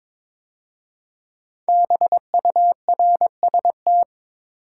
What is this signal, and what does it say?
Morse code, heard as a keyed tone, about 22 wpm, 700 Hz. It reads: BURST
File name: 5 Letter Words - First 3 Letters Given - 22wpm